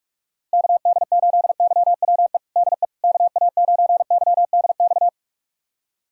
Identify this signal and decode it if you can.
Morse code, heard as a keyed tone, about 47 wpm, 700 Hz. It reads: KD8YWE DE KA9YDX
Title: USA Callsign DE Callsign 45wpm